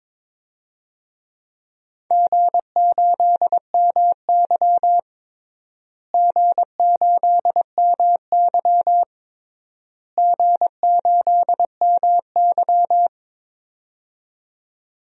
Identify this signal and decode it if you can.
Morse code, heard as a keyed tone, about 22 wpm, 700 Hz. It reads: G8MY G8MY G8MY
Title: DXCC-100 Call Signs - 4 Characters - Repeated 3x - 22wpm